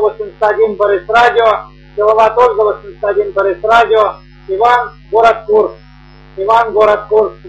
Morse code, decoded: none